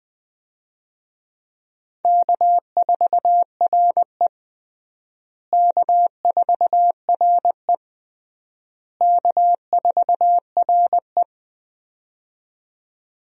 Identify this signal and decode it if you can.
Morse code, heard as a keyed tone, about 20 wpm, 700 Hz. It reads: K4RE K4RE K4RE
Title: USA Call Signs - 4 Characters - Repeated 3x - 20wpm